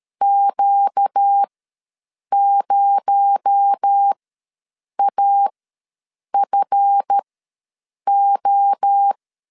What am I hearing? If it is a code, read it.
Morse code, heard as a keyed tone, about 13 wpm, spaced as for 7 wpm, 800 Hz. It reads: Q0AFO
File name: LEZIONI / 7 GF9Z / 7 GF ws 12wpm 3140